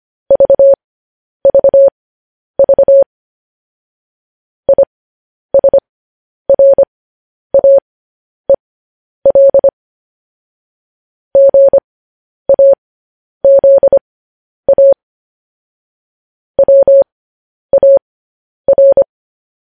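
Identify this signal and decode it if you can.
Morse code, heard as a keyed tone, about 25 wpm, spaced as for 10 wpm, 550 Hz. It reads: VVV ISRAEL GAZA WAR
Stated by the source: Morse code transmission of news headlines.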